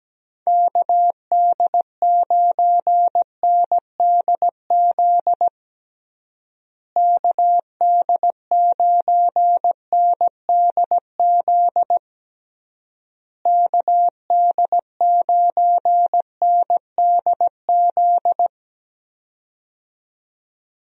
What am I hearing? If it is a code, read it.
Morse code, heard as a keyed tone, about 17 wpm, 700 Hz. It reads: KD9NDZ KD9NDZ KD9NDZ
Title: USA Call Signs - 6 Characters - Repeated 3x - 17wpm